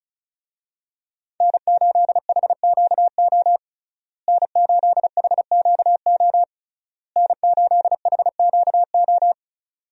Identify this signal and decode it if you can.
Morse code, heard as a keyed tone, about 36 wpm, 700 Hz. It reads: N8HQO N8HQO N8HQO
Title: USA Call Signs - 5 Characters - Repeated 3x - 35wpm